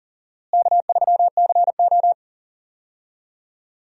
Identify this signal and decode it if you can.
Morse code, heard as a keyed tone, about 38 wpm, 700 Hz. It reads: K3CO
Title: USA Call Signs - 4 Characters - Repeated 2x - 40wpm